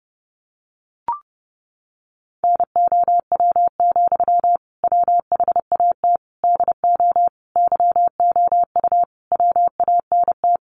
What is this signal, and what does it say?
Morse code, heard as a keyed tone, about 30 wpm, 700 Hz. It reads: NOW, WHAT DO YOU WANT
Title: Sentences from Top 200 Words 30wpm